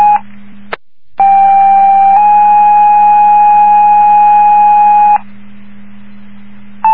Two-tone page, heard as a one-second tone, then a three-second tone, about 790 then 830 Hz. It is heard from 1.2 to 5.2 s.